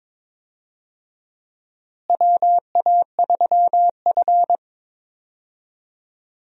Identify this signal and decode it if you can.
Morse code, heard as a keyed tone, about 22 wpm, 700 Hz. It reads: WA3F
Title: USA Call Signs - 4 Characters - Repeated 3x - 22wpm